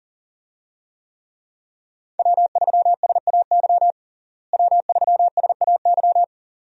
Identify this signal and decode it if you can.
Morse code, heard as a keyed tone, about 40 wpm, 700 Hz. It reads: W3SAY W3SAY
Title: USA Call Signs - 5 Characters - Repeated 2x - 40wpm